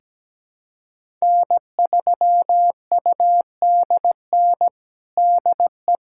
Morse code, heard as a keyed tone, about 17 wpm, 700 Hz. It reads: N3UDN DE
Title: USA Callsign DE Callsign - 5 Characters - 17wpm